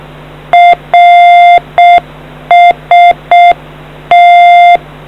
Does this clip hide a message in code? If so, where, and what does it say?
Yes: Morse code from 0.5 to 4.8 s, heard as a keyed tone, about 6 wpm, 700 Hz. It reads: RST